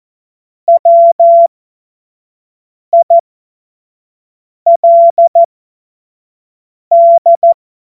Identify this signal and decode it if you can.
Morse code, heard as a keyed tone, about 14 wpm, spaced as for 5 wpm, 680 Hz. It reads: WILD
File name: Morsekurs / Witze / Witz 13. April 2021, 25 F70